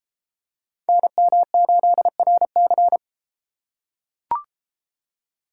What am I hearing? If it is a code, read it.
Morse code, heard as a keyed tone, about 30 wpm, 700 Hz. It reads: NM8RC